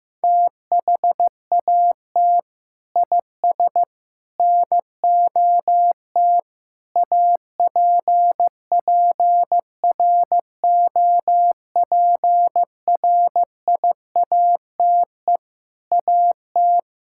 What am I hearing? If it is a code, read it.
Morse code, heard as a keyed tone, about 15 wpm, 700 Hz. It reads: THAT IS NOT APPROPRIATE AT
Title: Sentences from Top 1000 Words - Review Part 5 - 15wpm